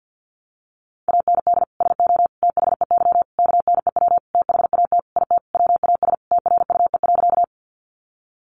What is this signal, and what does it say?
Morse code, heard as a keyed tone, about 50 wpm, 700 Hz. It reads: AND SO THEY KNEW THAT IT WAS TRUE.
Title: Sentences from Top 400 Words 50wpm